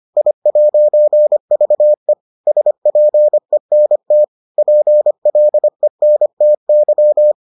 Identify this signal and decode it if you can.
Morse code, heard as a keyed tone, about 25 wpm, 600 Hz. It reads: I'VE SPENT PLENTY